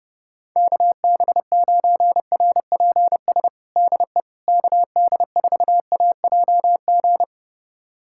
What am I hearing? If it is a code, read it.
Morse code, heard as a keyed tone, about 32 wpm, 700 Hz. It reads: KB9RPS DE KD4AJG
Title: USA Callsign DE Callsign - 6 Characters - 30wpm